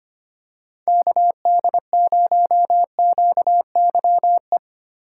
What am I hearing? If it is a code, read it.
Morse code, heard as a keyed tone, about 25 wpm, 700 Hz. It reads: KD0QYE
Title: USA Call Signs - 6 Characters - 3x Character Spacing - 25wpm